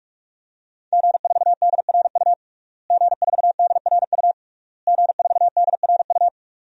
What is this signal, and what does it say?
Morse code, heard as a keyed tone, about 46 wpm, 700 Hz. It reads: G4DRU G4DRU G4DRU
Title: DXCC-100 Call Signs - 5 Characters - Repeated 3x - 45wpm